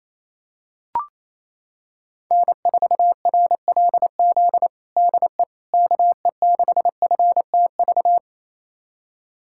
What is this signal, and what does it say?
Morse code, heard as a keyed tone, about 28 wpm, 700 Hz. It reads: N4RLZ DE KE6FTV
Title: USA Callsign DE Callsign 28wpm